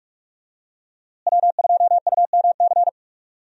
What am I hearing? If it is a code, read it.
Morse code, heard as a keyed tone, about 41 wpm, 700 Hz. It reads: W2UMC
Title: USA Call Signs - 5 Characters - Repeated 3x - 45wpm